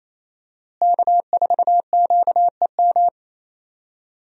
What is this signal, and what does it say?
Morse code, heard as a keyed tone, about 28 wpm, 700 Hz. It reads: K4QEM
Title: USA Call Signs - 5 Characters - Repeated 2x - 28wpm